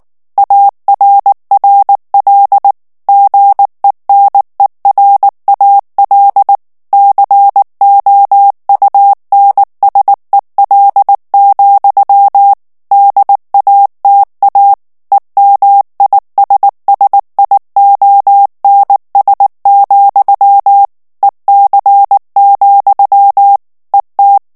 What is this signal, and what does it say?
Morse code, heard as a keyed tone, about 19 wpm, 800 Hz. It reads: ARRL GENERAL COUNSEL, DATA EMISSIONS, EC, ET
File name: morse mp3 / ANGLAIS texte clair 3 vitesse 18 mots minute